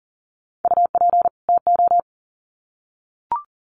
Morse code, heard as a keyed tone, about 40 wpm, 700 Hz. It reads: UP TO
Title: Sets of 2 Words 40wpm